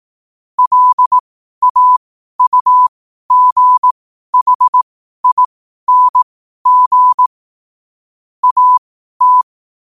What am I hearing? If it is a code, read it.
Morse code, heard as a keyed tone, about 18 wpm, spaced as for 12 wpm, 1000 Hz. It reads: LAUGHING AT
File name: qotdfiles / 12wpm